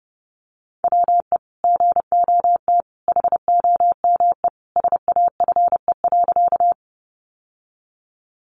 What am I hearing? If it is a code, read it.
Morse code, heard as a keyed tone, about 30 wpm, 700 Hz. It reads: WE GOT HOME SAFE.
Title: Sentences from Top 900 Words 30wpm